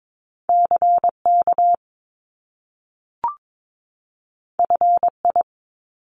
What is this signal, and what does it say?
Morse code, heard as a keyed tone, about 22 wpm, 700 Hz. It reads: CK FI
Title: Top Two Letter Combinations in Words 22wpm